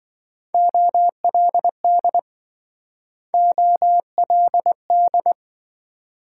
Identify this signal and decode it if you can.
Morse code, heard as a keyed tone, about 26 wpm, 700 Hz. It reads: OLD OLD